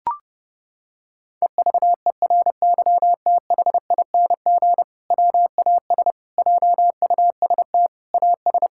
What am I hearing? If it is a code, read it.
Morse code, heard as a keyed tone, about 30 wpm, 700 Hz. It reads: EVERYTHING WAS JUST AS
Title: Sentences from Top 1600 Words 30wpm